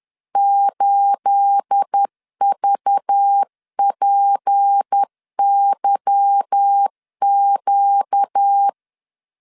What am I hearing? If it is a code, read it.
Morse code, heard as a keyed tone, about 10 wpm, 800 Hz. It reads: 8VPYQ
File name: LEZIONI / 9 2867 10wpm 2733